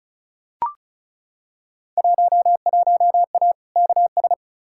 Morse code, heard as a keyed tone, about 34 wpm, 700 Hz. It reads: 11A KS
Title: Field Day Exchanges Encore - 35wpm